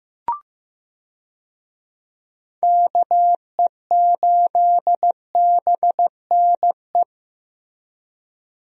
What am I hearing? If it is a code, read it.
Morse code, heard as a keyed tone, about 16 wpm, 700 Hz. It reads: KE8BNE
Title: USA Call Signs - 6 Characters - Repeated 2x - 15wpm